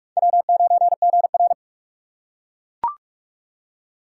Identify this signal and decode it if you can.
Morse code, heard as a keyed tone, about 39 wpm, 700 Hz. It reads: W9GR